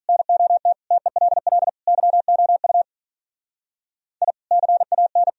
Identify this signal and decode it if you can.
Morse code, heard as a keyed tone, about 45 wpm, 700 Hz. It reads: NOT TELL YOU I CAN